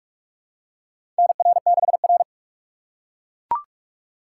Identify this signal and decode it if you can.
Morse code, heard as a keyed tone, about 41 wpm, 700 Hz. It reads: NR6R
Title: USA Call Signs - 4 Characters - 4x Character Spacing - 45wpm